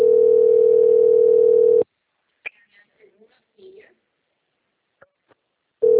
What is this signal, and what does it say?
Ringback tone (North America), 440 + 480 Hz, with speech.
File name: out- - - - - .6364